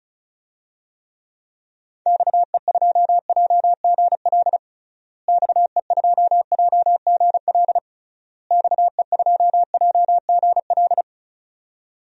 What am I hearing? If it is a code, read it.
Morse code, heard as a keyed tone, about 36 wpm, 700 Hz. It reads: XE2JGL XE2JGL XE2JGL
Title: DXCC-100 Call Signs - 6 Characters - Repeated 3x - 35wpm